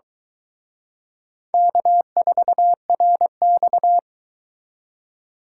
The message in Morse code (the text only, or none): K4RX